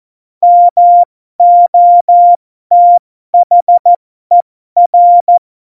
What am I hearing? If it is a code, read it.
Morse code, heard as a keyed tone, about 14 wpm, spaced as for 12 wpm, 700 Hz. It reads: MOTHER
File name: Morsekurs / Witze